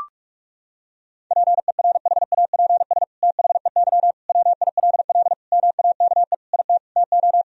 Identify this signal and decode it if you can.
Morse code, heard as a keyed tone, about 41 wpm, 700 Hz. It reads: PERHAPS THEY WILL MAKE IT TO
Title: Sentences from Top 1000 Words - Review Part 2 - 45wpm